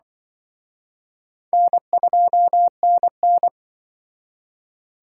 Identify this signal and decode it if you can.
Morse code, heard as a keyed tone, about 23 wpm, 700 Hz. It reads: N2NN
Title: Speed Racing - USA Call Signs 20wpm